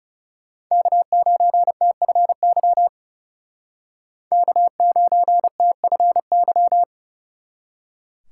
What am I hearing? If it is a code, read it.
Morse code, heard as a keyed tone, about 37 wpm, 700 Hz. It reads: K9TFY K9TFY
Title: Speed Racing - USA 1x3 Call Signs 30wpm